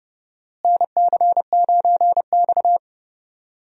Morse code, heard as a keyed tone, about 30 wpm, 700 Hz. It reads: NC9X
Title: USA Call Signs - 4 Characters - Repeated 3x - 30wpm